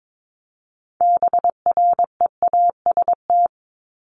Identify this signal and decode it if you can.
Morse code, heard as a keyed tone, about 22 wpm, 700 Hz. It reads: BREAST
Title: Top 1751-2000 Words 22wpm